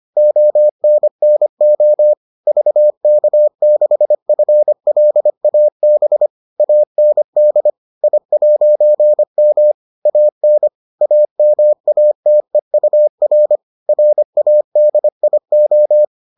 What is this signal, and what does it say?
Morse code, heard as a keyed tone, about 25 wpm, 600 Hz. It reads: ONNO VK6FLAB AND I'M AN AMATEUR RADIO